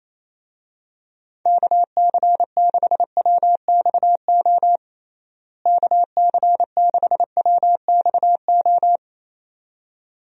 Morse code, heard as a keyed tone, about 28 wpm, 700 Hz. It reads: KC6WXO KC6WXO
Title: USA Call Signs - 6 Characters - Repeated 2x - 28wpm